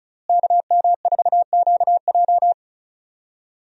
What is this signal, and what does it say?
Morse code, heard as a keyed tone, about 36 wpm, 700 Hz. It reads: KM4QJ
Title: USA Call Signs - 5 Characters - Repeated 2x - 35wpm